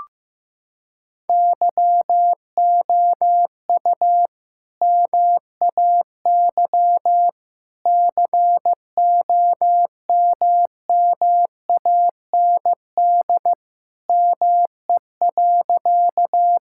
Morse code, heard as a keyed tone, about 15 wpm, 700 Hz. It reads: YOU MAY COMMAND ME.
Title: Sentences from Top 1600 Words 15wpm